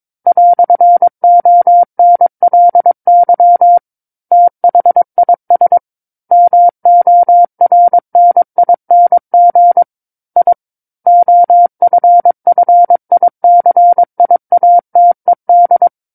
Morse code, heard as a keyed tone, about 22 wpm, 700 Hz. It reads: "ONLY THIS MORNING I OFFICIATED
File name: Book One / 13-Chapter 13 - 22 WPM